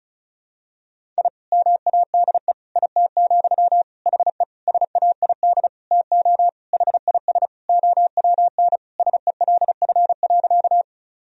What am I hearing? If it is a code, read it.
Morse code, heard as a keyed tone, about 36 wpm, 700 Hz. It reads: I MADE IT, HE SAID TO HIS OWN SELF.
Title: Sentences from Top 1000 Words - Review Part 1 - 35wpm